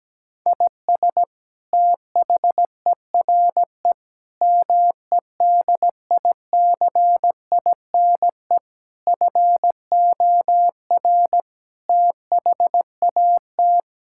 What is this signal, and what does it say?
Morse code, heard as a keyed tone, about 17 wpm, 700 Hz. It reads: IS THERE MEDICINE FOR THAT
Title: Sentences from Top 1000 Words - Review Part 6 - 17wpm